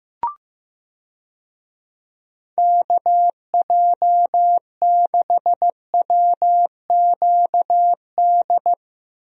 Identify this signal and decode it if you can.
Morse code, heard as a keyed tone, about 15 wpm, 700 Hz. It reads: KJ6WQD
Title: USA Call Signs - 6 Characters - 15wpm